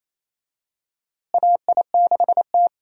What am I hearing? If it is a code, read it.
Morse code, heard as a keyed tone, about 28 wpm, 700 Hz. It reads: AI6T
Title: USA Call Signs - 4 Characters - Repeated 3x - 28wpm